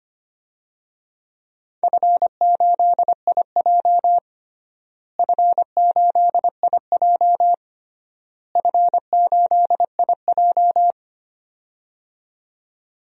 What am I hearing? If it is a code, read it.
Morse code, heard as a keyed tone, about 25 wpm, 700 Hz. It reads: F8IJ F8IJ F8IJ
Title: DXCC-100 Call Signs - 4 Characters - Repeated 3x - 25wpm